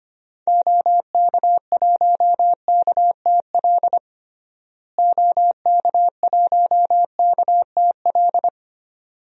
Morse code, heard as a keyed tone, about 25 wpm, 700 Hz. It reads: OK1KTL OK1KTL
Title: DXCC-100 Call Signs - 6 Characters - Repeated 2x - 25wpm